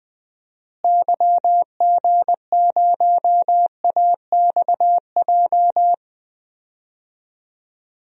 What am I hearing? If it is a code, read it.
Morse code, heard as a keyed tone, about 20 wpm, 700 Hz. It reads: YG0AXJ
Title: DXCC-100 Call Signs - 6 Characters - Repeated 3x - 20wpm